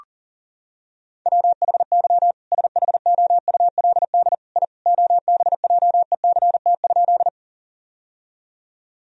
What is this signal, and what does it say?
Morse code, heard as a keyed tone, about 37 wpm, 700 Hz. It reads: WHY SHOULD I OBJECT?
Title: Sentences from Top 1000 Words - Review Part 3 - 40wpm